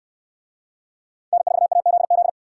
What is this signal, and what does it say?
Morse code, heard as a keyed tone, about 65 wpm, 700 Hz. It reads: N4AZZ